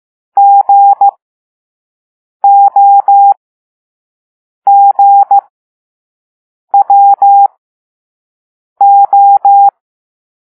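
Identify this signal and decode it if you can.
Morse code, heard as a keyed tone, about 15 wpm, spaced as for 5 wpm, 800 Hz. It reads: GOGWO
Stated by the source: Morse code lesson 5 drill of random five-letter groups using all letters learned so far.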